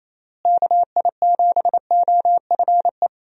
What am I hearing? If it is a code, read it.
Morse code, heard as a keyed tone, about 28 wpm, 700 Hz. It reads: KI7OFE
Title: USA Call Signs - 6 Characters - Repeated 3x - 28wpm